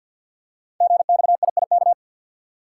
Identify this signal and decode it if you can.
Morse code, heard as a keyed tone, about 51 wpm, 700 Hz. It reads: GXIIK
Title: Code Groups - 5 Letters - 50wpm